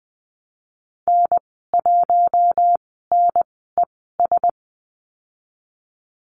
Morse code, heard as a keyed tone, about 20 wpm, spaced as for 14 wpm, 700 Hz. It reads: N1NES
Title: 076 Lesson 13g - Letter L - Call Signs 20wpm